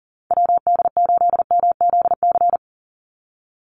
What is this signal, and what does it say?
Morse code, heard as a keyed tone, about 40 wpm, 700 Hz. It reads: WD8MZC
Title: USA Call Signs 40wpm